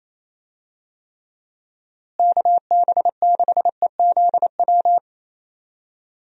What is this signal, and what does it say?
Morse code, heard as a keyed tone, about 27 wpm, 700 Hz. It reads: KB6EZW
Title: USA Call Signs - 6 Characters - Repeated 2x - 28wpm